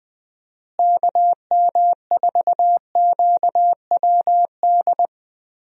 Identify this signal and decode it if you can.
Morse code, heard as a keyed tone, about 20 wpm, 700 Hz. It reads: KM4QWD
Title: USA Call Signs - 6 Characters - Repeated 3x - 20wpm